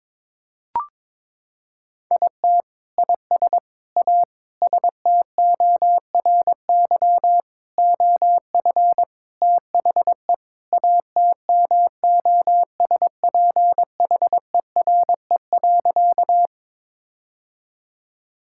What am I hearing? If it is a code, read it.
Morse code, heard as a keyed tone, about 22 wpm, 700 Hz. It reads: IT IS A STORY OF THE ATMOSPHERE.
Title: Sentences from Top 1600 Words 22wpm